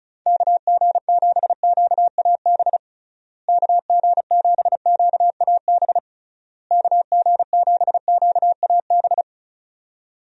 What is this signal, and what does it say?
Morse code, heard as a keyed tone, about 36 wpm, 700 Hz. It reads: KG7QAB KG7QAB KG7QAB
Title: USA Call Signs - 6 Characters - Repeated 3x - 35wpm